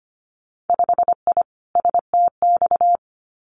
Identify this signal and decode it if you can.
Morse code, heard as a keyed tone, about 25 wpm, 690 Hz. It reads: 5I STX
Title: Winter Field Day Exchanges Encore 25wpm